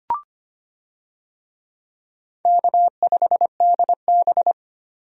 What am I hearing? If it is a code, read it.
Morse code, heard as a keyed tone, about 25 wpm, 700 Hz. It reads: K5DB
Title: USA Call Signs - 4 Characters - 25wpm